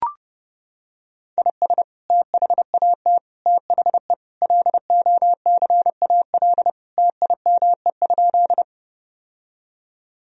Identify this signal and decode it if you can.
Morse code, heard as a keyed tone, about 30 wpm, 700 Hz. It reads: IS THAT THE LOCAL TIME?
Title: Sentences from Top 1000 Words - Review Part 3 - 30wpm